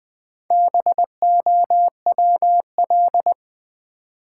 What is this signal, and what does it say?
Morse code, heard as a keyed tone, about 20 wpm, 700 Hz. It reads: BOWL
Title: Sentences from Top 1200 Words - New Word Drill - 20wpm